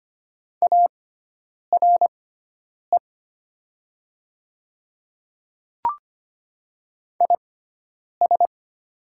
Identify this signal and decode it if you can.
Morse code, heard as a keyed tone, about 25 wpm, spaced as for 9 wpm, 700 Hz. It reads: ARE IS
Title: 6x Character Spacing - Top 200 Words - 25wpm